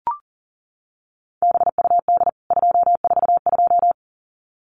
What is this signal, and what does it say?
Morse code, heard as a keyed tone, about 40 wpm, 700 Hz. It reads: BUD 242
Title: CWT Exchange 40wpm